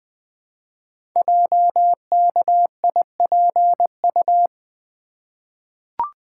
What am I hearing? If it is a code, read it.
Morse code, heard as a keyed tone, about 20 wpm, 700 Hz. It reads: JKIPU
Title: Code Groups - 5 Letters - 20wpm